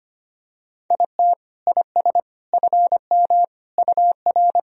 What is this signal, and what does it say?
Morse code, heard as a keyed tone, about 25 wpm, 700 Hz. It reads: IT IS FM UR
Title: Sentences from Top 300 Words and Abbreviations 25wpm